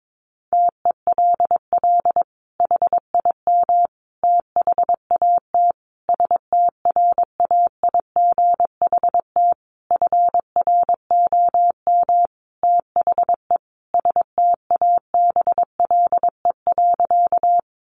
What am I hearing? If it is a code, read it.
Morse code, heard as a keyed tone, about 22 wpm, 700 Hz. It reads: TELL HIM THAT STRAIGHT FROM THE STABLE.
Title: Sentences from Top 1500 Words 22wpm